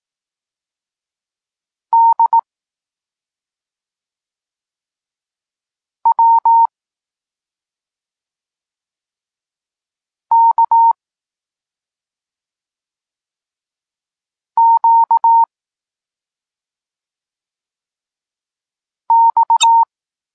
Morse code, heard as a keyed tone, about 18 wpm, 920 Hz. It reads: D W K Q X